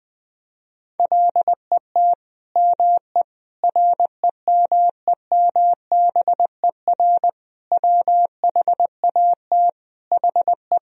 Morse code, heard as a keyed tone, about 21 wpm, 700 Hz. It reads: LET ME REMEMBER WHAT HE
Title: Sentences from Top 400 Words and Abbreviations 20wpm